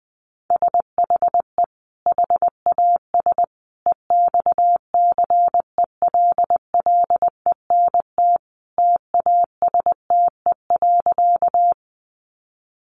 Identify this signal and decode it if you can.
Morse code, heard as a keyed tone, about 20 wpm, 700 Hz. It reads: SHE HAS EXCELLENT TASTE.
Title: Sentences from Top 1100 Words 20wpm